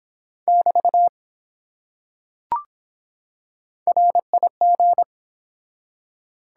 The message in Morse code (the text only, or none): <BT> RIG